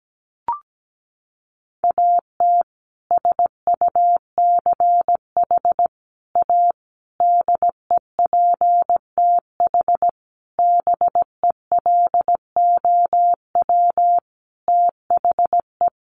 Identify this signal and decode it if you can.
Morse code, heard as a keyed tone, about 17 wpm, 700 Hz. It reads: AT SUCH A DEPTH BELOW THE
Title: Sentences from Top 1200 Words 17wpm